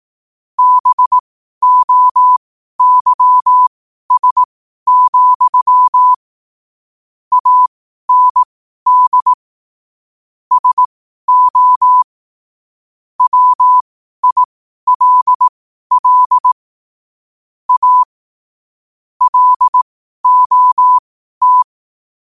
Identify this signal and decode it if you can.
Morse code, heard as a keyed tone, about 18 wpm, spaced as for 12 wpm, 1000 Hz. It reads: BOYS, AND SO WILL A LOT
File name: qotdfiles / 12wpm